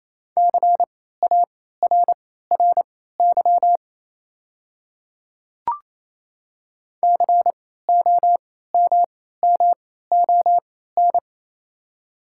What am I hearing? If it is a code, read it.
Morse code, heard as a keyed tone, about 28 wpm, spaced as for 16 wpm, 700 Hz. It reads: CARRY COMMON
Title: 3x Character Spacing - Top 500 Words - 28wpm